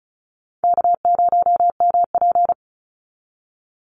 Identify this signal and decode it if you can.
Morse code, heard as a keyed tone, about 34 wpm, 700 Hz. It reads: K0MP
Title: NA SOTA Chaser Callsigns 35wpm